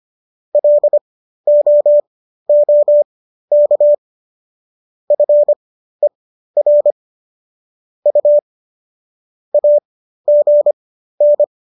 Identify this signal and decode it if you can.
Morse code, heard as a keyed tone, about 25 wpm, spaced as for 13 wpm, 590 Hz. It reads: LOOK FER U AGN